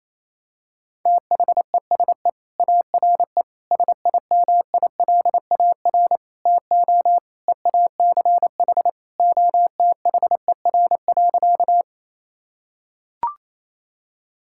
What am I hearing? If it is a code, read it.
Morse code, heard as a keyed tone, about 28 wpm, 700 Hz. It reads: THESE ARE SIMILAR TO EACH OTHER.
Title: Sentences from Top 1000 Words - Review Part 5 - 28wpm